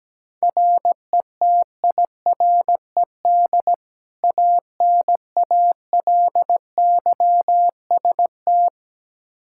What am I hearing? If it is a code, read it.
Morse code, heard as a keyed tone, about 17 wpm, 700 Hz. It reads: RETIRED ANALYST
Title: Practice QSO - Exchange 3 - More Info - 17wpm